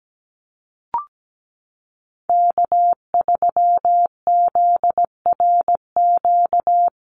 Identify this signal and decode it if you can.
Morse code, heard as a keyed tone, about 17 wpm, 700 Hz. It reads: K3ZRQ
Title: USA Call Signs Repeated 3x 17wpm